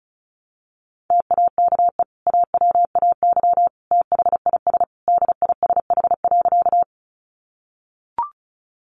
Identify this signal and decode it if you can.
Morse code, heard as a keyed tone, about 34 wpm, 700 Hz. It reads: TAKE AWAY THIS DISH.
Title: Sentences from Top 1300 Words 35wpm